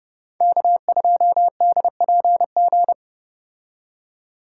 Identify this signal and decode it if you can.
Morse code, heard as a keyed tone, about 33 wpm, 700 Hz. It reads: K2DPG